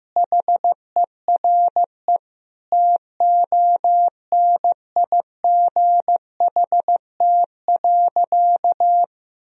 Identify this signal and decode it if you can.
Morse code, heard as a keyed tone, about 15 wpm, 700 Hz. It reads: HERE TONIGHT.
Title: Sentences from Top 1000 Words - Review Part 3 - 15wpm